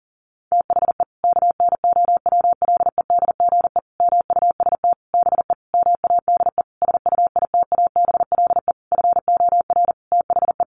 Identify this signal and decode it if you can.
Morse code, heard as a keyed tone, about 40 wpm, 700 Hz. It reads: THE KNOWLEDGE MUST BE MADE SUITABLE FOR THE
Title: Sentences from Top 1400 Words 40wpm